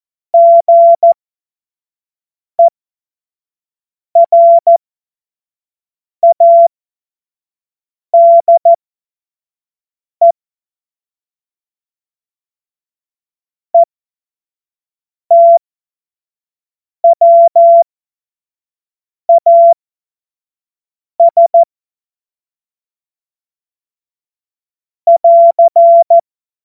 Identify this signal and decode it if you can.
Morse code, heard as a keyed tone, about 14 wpm, spaced as for 5 wpm, 680 Hz. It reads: GERADE ETWAS <AR>
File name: Morsekurs / Witze / Witz 03. Januar 2023 25 F70